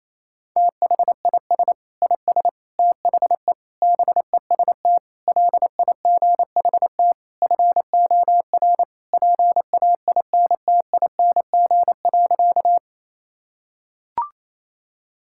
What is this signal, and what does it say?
Morse code, heard as a keyed tone, about 28 wpm, 690 Hz. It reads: THIS IS THE BEST LIGHT FOR PAINTING.
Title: Sentences from Top 1000 Words - Review Part 6 - 28wpm